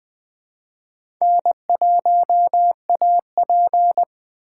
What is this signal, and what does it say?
Morse code, heard as a keyed tone, about 21 wpm, 700 Hz. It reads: N1AP
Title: Speed Racing - USA 1x2 Call Signs 17wpm